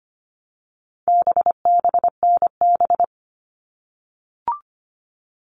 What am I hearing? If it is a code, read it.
Morse code, heard as a keyed tone, about 25 wpm, 700 Hz. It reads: BBNB
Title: All Cut Numbers - 4-Digit 25wpm